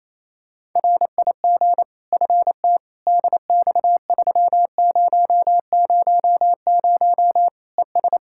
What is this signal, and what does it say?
Morse code, heard as a keyed tone, about 28 wpm, 700 Hz. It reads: RIG FT DX3000 ES
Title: Practice QSO - Exchange 2 - Info - 28wpm